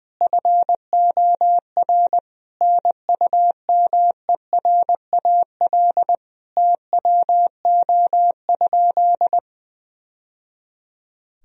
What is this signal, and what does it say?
Morse code, heard as a keyed tone, about 21 wpm, 700 Hz. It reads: FOR NUMERAL TWO?
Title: Sentences from Top 1000 Words - Review Part 2 - 20wpm